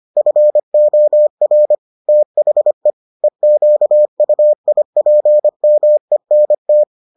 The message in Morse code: FOR THE EQUIPMENT

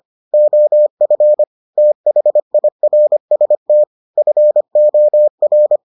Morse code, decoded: OF THIRST FOR

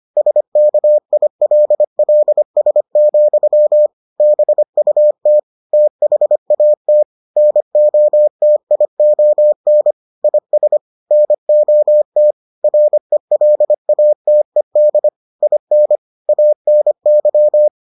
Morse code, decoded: SKILLS, BUT THAT NOTION IS NOT RELATED IN ANY

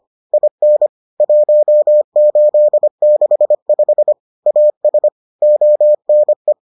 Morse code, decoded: IN 1865 AS ONE

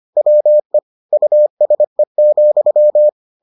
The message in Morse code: WE USE,

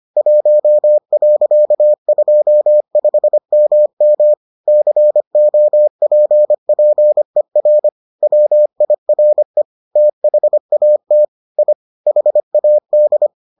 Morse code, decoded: 1.25MM COPPER WIRE THAT I HAD

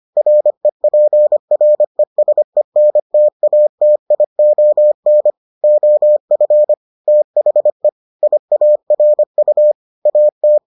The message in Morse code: REPRESENTATION OF THE IARU AT

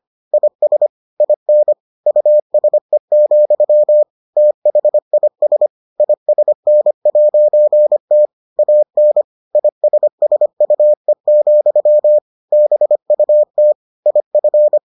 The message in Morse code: IS IN USE, THIS ISN'T AN ISSUE, BUT IF

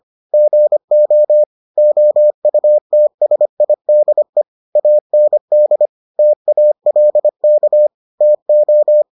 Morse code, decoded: GO OUTSIDE AND TALK TO

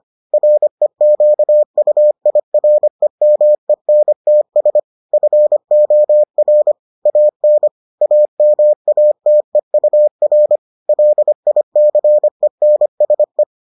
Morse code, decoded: REQUIREMENTS FOR AN AMATEUR LICENSE

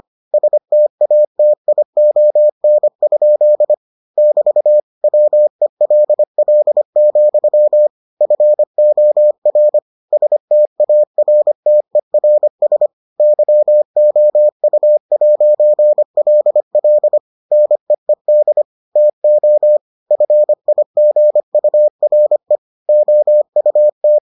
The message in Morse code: STATION? <BT> WELL, FOR STARTERS YOU'LL NEED TO FIGURE OUT

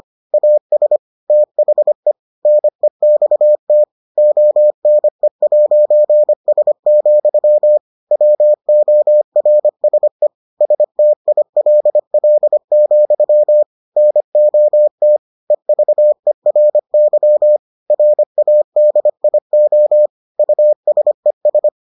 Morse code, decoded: AS THE NEXT ONE'S, WORSE STILL, NOT EVERY RADIO USES

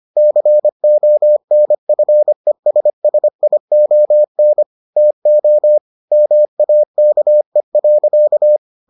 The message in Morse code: CONFESSION TO MAKE.

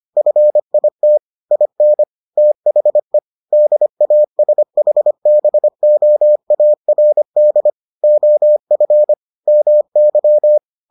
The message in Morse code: FIT IN THE DASHBOARD OF MY

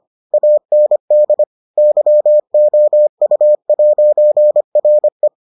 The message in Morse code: AND YOU'RE

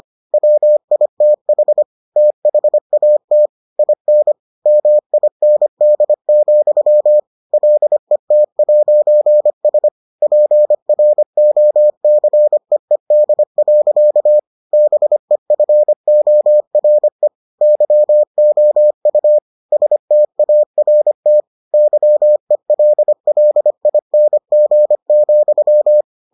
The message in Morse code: WITH THAT IN MIND, LET'S PROCEED. BEFORE YOU START YELLING,

